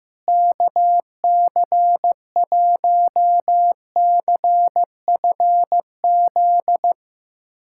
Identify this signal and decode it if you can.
Morse code, heard as a keyed tone, about 15 wpm, 700 Hz. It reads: KC1CFZ
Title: USA Callsign DE Callsign 15wpm